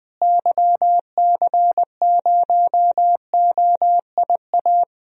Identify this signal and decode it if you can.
Morse code, heard as a keyed tone, about 21 wpm, 700 Hz. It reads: YC0OIA
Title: DXCC-100 Call Signs - 6 Characters - Repeated 2x - 20wpm